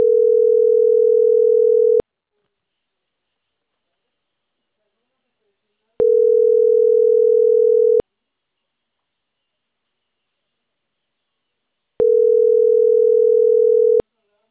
Ringback tone (North America), 440 + 480 Hz.